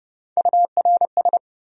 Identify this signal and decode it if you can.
Morse code, heard as a keyed tone, about 37 wpm, 700 Hz. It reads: URS
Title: Speed-Racing Abbreviations - 30wpm